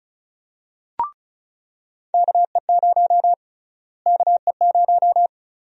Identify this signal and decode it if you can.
Morse code, heard as a keyed tone, about 36 wpm, 700 Hz. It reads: KE0 KE0
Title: USA Call Sign Prefixes - Repeated 2x - 35wpm